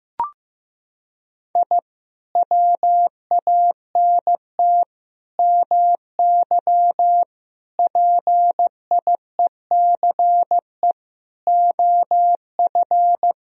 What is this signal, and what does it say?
Morse code, heard as a keyed tone, about 15 wpm, 700 Hz. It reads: I WANT MY PIECE OF
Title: Sentences from Top 1000 Words - Review Part 2 - 15wpm